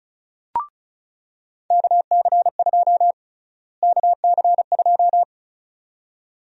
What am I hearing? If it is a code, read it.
Morse code, heard as a keyed tone, about 36 wpm, 700 Hz. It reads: KC2 KC2
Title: USA Call Sign Prefixes - Repeated 2x - 35wpm